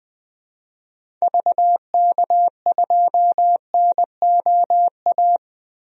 Morse code, heard as a keyed tone, about 21 wpm, 700 Hz. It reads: VK2NOA
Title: DXCC-100 Call Signs - 6 Characters - Repeated 2x - 20wpm